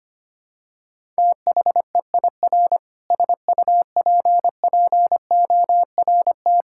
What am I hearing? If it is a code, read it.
Morse code, heard as a keyed tone, about 25 wpm, 700 Hz. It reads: THEIR SUPPORT
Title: Sentences from Top 1000 Words - Review Part 6 - 25wpm